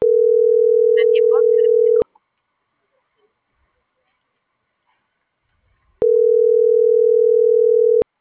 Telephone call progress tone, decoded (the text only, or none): ringback tone